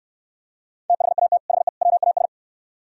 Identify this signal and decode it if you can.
Morse code, heard as a keyed tone, about 66 wpm, 700 Hz. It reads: THAT HE WAS